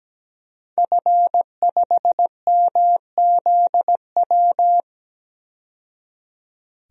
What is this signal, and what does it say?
Morse code, heard as a keyed tone, about 17 wpm, 700 Hz. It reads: F5MZW
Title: DXCC-100 Call Signs - 5 Characters - 17wpm